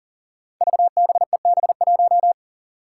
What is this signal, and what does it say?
Morse code, heard as a keyed tone, about 41 wpm, 700 Hz. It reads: VBEB1
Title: Five Challenging Letter-Number Combinations 40wpm